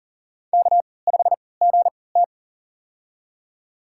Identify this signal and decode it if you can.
Morse code, heard as a keyed tone, about 38 wpm, spaced as for 22 wpm, 700 Hz. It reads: K5GT
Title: USA Call Signs - 4 Characters - 3x Character Spacing - 40wpm